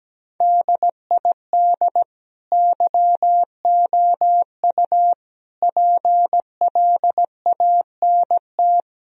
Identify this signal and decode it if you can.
Morse code, heard as a keyed tone, about 17 wpm, 700 Hz. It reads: DID YOU PLANT
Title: Sentences from Top 1000 Words - Review Part 1 - 17wpm